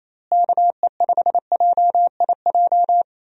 Morse code, heard as a keyed tone, about 27 wpm, 700 Hz. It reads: KE5JIJ
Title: USA Call Signs - 6 Characters - Repeated 3x - 28wpm